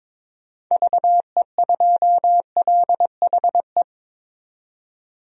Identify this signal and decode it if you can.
Morse code, heard as a keyed tone, about 22 wpm, 700 Hz. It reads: VE2LHE
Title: DXCC-100 Call Signs - 6 Characters - 22wpm